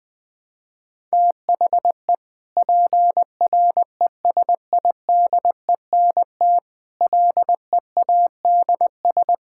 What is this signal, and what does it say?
Morse code, heard as a keyed tone, about 21 wpm, 700 Hz. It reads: THE PRESIDENT LEADS
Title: Sentences with 5-Word Phrases - Spoken Beginning - 20wpm